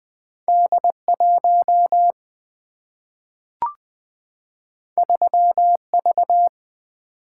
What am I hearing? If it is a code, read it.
Morse code, heard as a keyed tone, about 20 wpm, 700 Hz. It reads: D 1 3 V
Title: Single-Letter Single-Number 20wpm